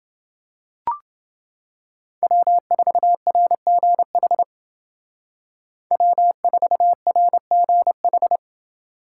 Morse code, heard as a keyed tone, about 29 wpm, 700 Hz. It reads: W4RGH W4RGH